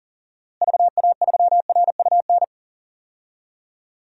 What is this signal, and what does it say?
Morse code, heard as a keyed tone, about 37 wpm, 700 Hz. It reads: VA3RUN
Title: DXCC-100 Call Signs - 6 Characters - Repeated 3x - 40wpm